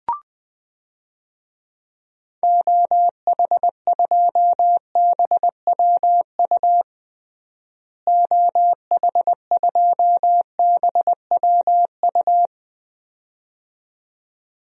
Morse code, heard as a keyed tone, about 21 wpm, 700 Hz. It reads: OH2BWU OH2BWU
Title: DXCC-100 Call Signs - 6 Characters - Repeated 2x - 20wpm